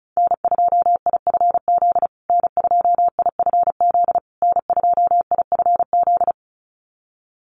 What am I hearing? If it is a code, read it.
Morse code, heard as a keyed tone, about 34 wpm, 700 Hz. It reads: N2IFZ N2IFZ N2IFZ
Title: USA Call Signs Repeated 3x 35wpm